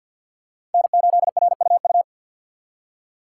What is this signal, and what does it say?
Morse code, heard as a keyed tone, about 52 wpm, 700 Hz. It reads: N8RUU
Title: USA Call Signs - 6x Character Spacing - 50wpm